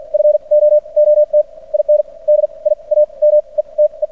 Morse code, heard as a keyed tone, about 56 wpm, spaced as for 36 wpm, 600 Hz. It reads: FOOT IN DIAMETE
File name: archived / CRSnet